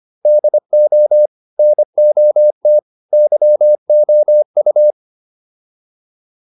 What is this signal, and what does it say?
Morse code, heard as a keyed tone, about 25 wpm, 600 Hz. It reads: DO NOT YOU